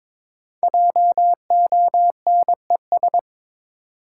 Morse code, heard as a keyed tone, about 22 wpm, 700 Hz. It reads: JONES
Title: Sets of 5 Words - New Word Drill - 22wpm